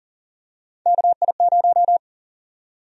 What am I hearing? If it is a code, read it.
Morse code, heard as a keyed tone, about 40 wpm, 700 Hz. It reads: KI0
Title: USA Call Sign Prefixes - 40wpm